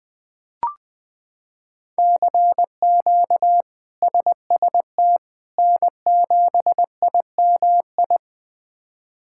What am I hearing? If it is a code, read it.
Morse code, heard as a keyed tone, about 21 wpm, 700 Hz. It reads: CQ SST N7IMI
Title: K1USN SST QSO Practice - 20wpm